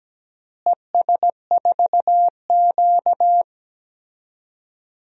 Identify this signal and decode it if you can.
Morse code, heard as a keyed tone, about 17 wpm, 700 Hz. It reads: ES4Q
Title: DXCC-100 Call Signs - 4 Characters - Repeated 2x - 17wpm